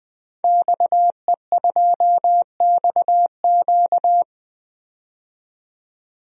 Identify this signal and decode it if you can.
Morse code, heard as a keyed tone, about 21 wpm, 700 Hz. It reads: XE2XQ
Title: DXCC-100 Call Signs - 5 Characters - 20wpm